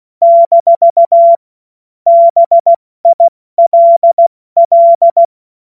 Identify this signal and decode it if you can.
Morse code, heard as a keyed tone, about 16 wpm, 680 Hz. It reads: - BILL
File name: Morsekurs/Witze